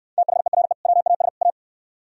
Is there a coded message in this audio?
Yes: Morse code from 0.2 to 1.5 s, heard as a keyed tone, about 66 wpm, 700 Hz. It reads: THERE WAS A